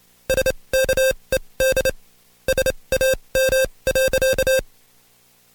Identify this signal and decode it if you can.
Morse code, heard as a keyed tone, about 27 wpm, spaced as for 22 wpm, 530 Hz. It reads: SKED SAM.
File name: Code Quick / AudioDownld2 / Seg137-Stepper10 21WPM